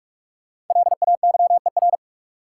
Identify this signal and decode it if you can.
Morse code, heard as a keyed tone, about 47 wpm, 700 Hz. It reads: LAYER
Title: Sentences from Top 1600 Words - New Word Drill Alt - 45wpm